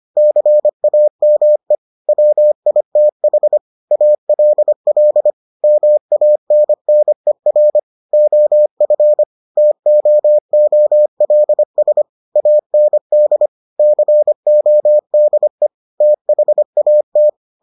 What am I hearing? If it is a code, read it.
Morse code, heard as a keyed tone, about 25 wpm, 600 Hz. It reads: CAME WITH ALL MANNER OF TOOLS AND CODE THAT